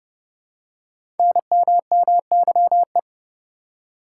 Morse code, heard as a keyed tone, about 30 wpm, 700 Hz. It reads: NMMYE
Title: Code Groups - 5 Letters - 30wpm